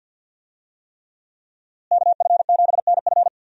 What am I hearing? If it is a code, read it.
Morse code, heard as a keyed tone, about 52 wpm, 700 Hz. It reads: KF7NF
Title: USA Call Signs - 5 Characters - Repeated 3x - 50wpm